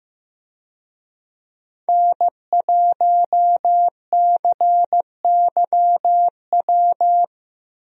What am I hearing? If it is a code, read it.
Morse code, heard as a keyed tone, about 15 wpm, 700 Hz. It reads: N1CYW
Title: USA Call Signs - 5 Characters - 15wpm